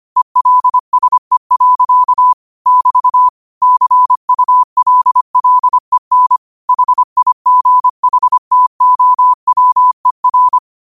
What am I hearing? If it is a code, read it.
Morse code, heard as a keyed tone, about 25 wpm, 1000 Hz. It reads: ELSE. <BT> CULLEN HIGHTOWER
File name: qotdfiles / 25wpm